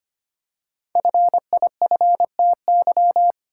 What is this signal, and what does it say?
Morse code, heard as a keyed tone, about 25 wpm, 700 Hz. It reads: FIFTY
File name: 5 Letter Words - First 3 Letters Given - 25wpm